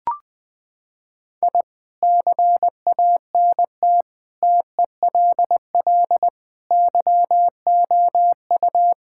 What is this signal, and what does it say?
Morse code, heard as a keyed tone, about 20 wpm, 700 Hz. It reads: I CANT TELL YOU
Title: Sentences with 8-Word Phrases - 20wpm